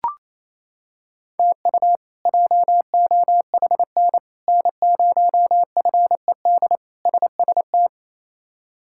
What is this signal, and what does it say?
Morse code, heard as a keyed tone, about 28 wpm, 700 Hz. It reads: TU JOHN N0FED SST
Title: K1USN SST QSO Practice - 28wpm